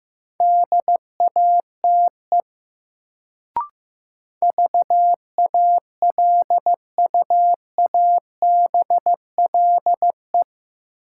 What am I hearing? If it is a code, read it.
Morse code, heard as a keyed tone, about 15 wpm, 700 Hz. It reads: DATE VALUABLE
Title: Sentences from Top 1100 Words - New Word Drill Alt - 15wpm